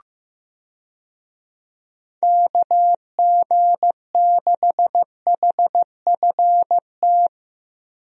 Morse code, heard as a keyed tone, about 16 wpm, 700 Hz. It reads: KG6HFT
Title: USA Call Signs - 6 Characters - Repeated 3x - 15wpm